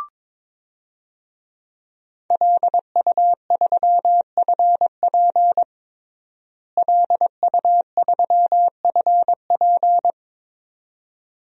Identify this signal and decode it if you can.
Morse code, heard as a keyed tone, about 22 wpm, 700 Hz. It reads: LU3FP LU3FP
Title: DXCC-100 Call Signs - 5 Characters - Repeated 2x - 22wpm